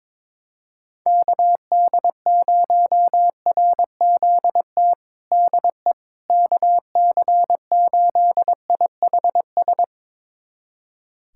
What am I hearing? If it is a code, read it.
Morse code, heard as a keyed tone, about 22 wpm, 700 Hz. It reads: KD0RZT DE KC8IHS
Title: USA Callsign DE Callsign 22wpm